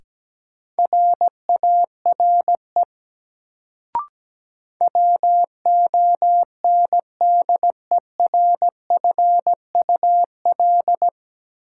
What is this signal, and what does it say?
Morse code, heard as a keyed tone, about 17 wpm, 700 Hz. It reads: RARE WONDERFUL
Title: Sentences from Top 1100 Words - New Word Drill Alt - 17wpm